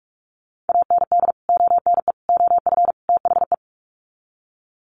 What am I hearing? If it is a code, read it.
Morse code, heard as a keyed tone, about 44 wpm, 700 Hz. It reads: AND ONE OF THE
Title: Sets of 4 Words 45wpm